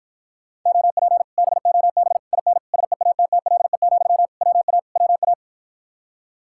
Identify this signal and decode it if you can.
Morse code, heard as a keyed tone, about 54 wpm, 700 Hz. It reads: OP BOB IN SEATTLE, WA WA